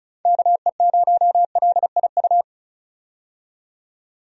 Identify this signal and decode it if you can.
Morse code, heard as a keyed tone, about 36 wpm, 700 Hz. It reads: KE0LIU
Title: USA Call Signs - 6 Characters - 35wpm